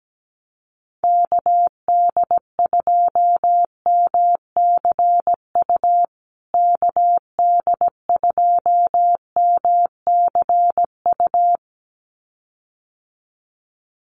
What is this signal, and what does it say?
Morse code, heard as a keyed tone, about 17 wpm, 700 Hz. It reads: KD2MCU KD2MCU
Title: USA Call Signs Repeated 2x 17wpm